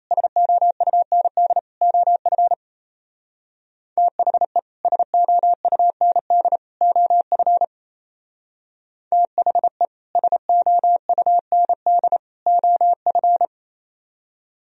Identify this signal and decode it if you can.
Morse code, heard as a keyed tone, about 33 wpm, 700 Hz. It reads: SOUND OF THE SOUND OF THE SOUND OF
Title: Speed-Racing - Sets of 3 Words 28wpm